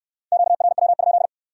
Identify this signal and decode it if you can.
Morse code, heard as a keyed tone, about 65 wpm, 700 Hz. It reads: QRW?